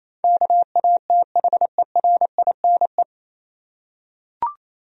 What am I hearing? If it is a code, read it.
Morse code, heard as a keyed tone, about 27 wpm, 700 Hz. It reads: KATHERINE
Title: USA Common Names - 28wpm